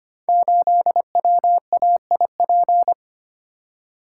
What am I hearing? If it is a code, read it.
Morse code, heard as a keyed tone, about 25 wpm, 700 Hz. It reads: 8WAIP
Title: Code Groups - 5 Letter:Numbers - 25wpm